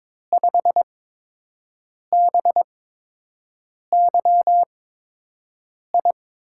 Morse code, heard as a keyed tone, about 22 wpm, spaced as for 6 wpm, 700 Hz. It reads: 5BYI
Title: USA Call Signs - 5 Characters - 8x Character Spacing - 22wpm